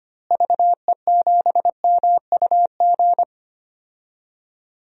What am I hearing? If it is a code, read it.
Morse code, heard as a keyed tone, about 25 wpm, 700 Hz. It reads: VE7MUG
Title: DXCC-100 Call Signs - 6 Characters - Repeated 3x - 25wpm